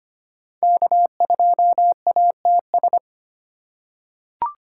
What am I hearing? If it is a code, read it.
Morse code, heard as a keyed tone, about 25 wpm, 700 Hz. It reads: K2ATS
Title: USA Call Signs - 5 Characters - 4x Character Spacing - 25wpm